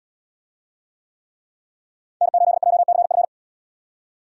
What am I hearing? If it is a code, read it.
Morse code, heard as a keyed tone, about 73 wpm, 700 Hz. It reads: N8JCF